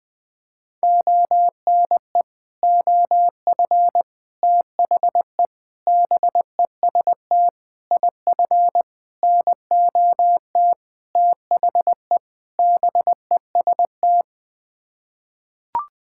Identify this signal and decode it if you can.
Morse code, heard as a keyed tone, about 20 wpm, 700 Hz. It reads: ONE OF THE BEST IF NOT THE BEST
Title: Sets of 8 Words - 20wpm